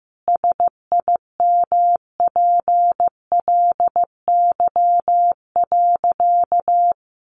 Morse code, heard as a keyed tone, about 15 wpm, 700 Hz. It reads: SIMPLY.
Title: Sentences from Top 600 Words 15wpm